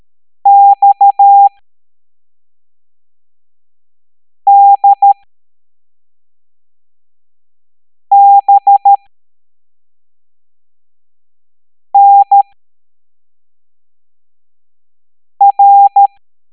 Morse code, heard as a keyed tone, about 13 wpm, spaced as for 3 wpm, 800 Hz. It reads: XDBNR